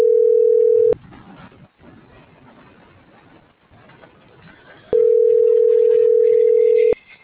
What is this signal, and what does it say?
Ringback tone (North America), 440 + 480 Hz.